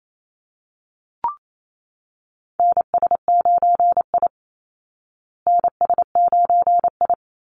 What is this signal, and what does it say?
Morse code, heard as a keyed tone, about 28 wpm, 700 Hz. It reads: NS9I NS9I
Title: Call Signs Repeated 3x - 28wpm